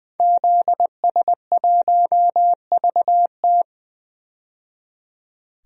Morse code, heard as a keyed tone, about 20 wpm, 700 Hz. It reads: ZS1VT
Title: DXCC-100 Call Signs - 5 Characters - 20wpm